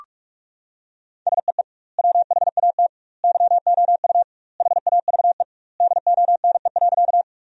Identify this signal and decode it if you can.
Morse code, heard as a keyed tone, about 46 wpm, 700 Hz. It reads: SEE WHAT YOU HAVE DONE.
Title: 2x Word-Spacing - Sentences from Top 500 Words - 45wpm